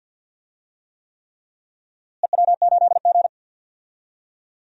Morse code, heard as a keyed tone, about 53 wpm, 700 Hz. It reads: EW8G
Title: DXCC-100 Call Signs - 4 Characters - Repeated 3x - 50wpm